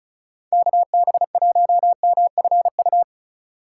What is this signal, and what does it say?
Morse code, heard as a keyed tone, about 36 wpm, 700 Hz. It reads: KB1MFU
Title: USA Call Signs - 3x Character Spacing - 35wpm